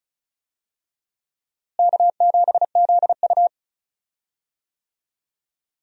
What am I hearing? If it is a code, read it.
Morse code, heard as a keyed tone, about 34 wpm, 700 Hz. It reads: K7ZU
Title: USA Call Signs - 4 Characters - Repeated 2x - 35wpm